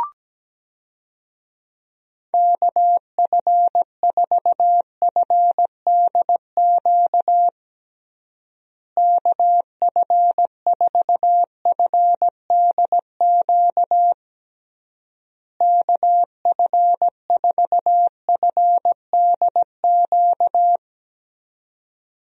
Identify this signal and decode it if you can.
Morse code, heard as a keyed tone, about 17 wpm, 700 Hz. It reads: KF4FDQ KF4FDQ KF4FDQ
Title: USA Call Signs - 6 Characters - Repeated 3x - 17wpm